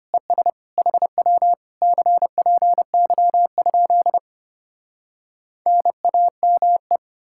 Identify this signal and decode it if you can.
Morse code, heard as a keyed tone, about 33 wpm, 700 Hz. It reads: ES HW CPY? NAME